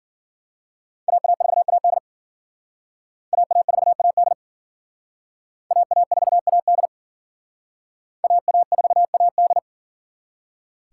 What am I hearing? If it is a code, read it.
Morse code, heard as a keyed tone, about 50 wpm, 700 Hz. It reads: AA4AD AA4AD AA4AD AA4AD
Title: Speed Racing - USA 2x2 Call Signs 40wpm